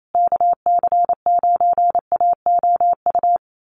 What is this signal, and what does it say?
Morse code, heard as a keyed tone, about 28 wpm, 700 Hz. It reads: KC9AOU
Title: USA Call Signs Repeated 3x 28wpm